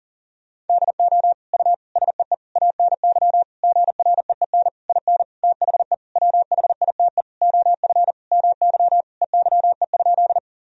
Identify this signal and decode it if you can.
Morse code, heard as a keyed tone, about 37 wpm, 700 Hz. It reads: DO U SEE ANY GREEN IN THE WHITE OF MY EYE?
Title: Sentences from Top 500 Words and Abbreviations 40wpm